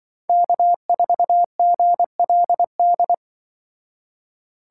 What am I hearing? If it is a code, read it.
Morse code, heard as a keyed tone, about 25 wpm, 700 Hz. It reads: K4GLD
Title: Speed Racing - USA 1x3 Call Signs 20wpm